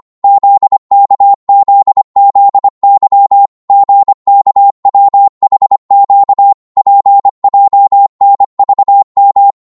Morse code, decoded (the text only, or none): ZKZZY GKWHQ PJNVM